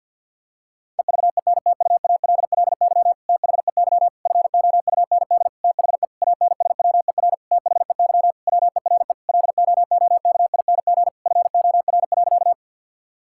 Morse code, heard as a keyed tone, about 50 wpm, 700 Hz. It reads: EVENTUALLY THEY FOUND THE ANSWER THEY WERE LOOKING FOR.
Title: Sentences from Top 1000 Words - Review Part 6 - 50wpm